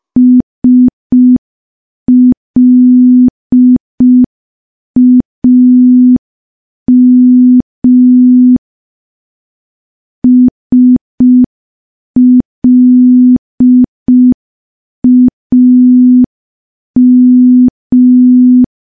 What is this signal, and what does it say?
Morse code, heard as a keyed tone, about 5 wpm, 250 Hz. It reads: SLAM SLAM
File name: morse / morse-training-05WPM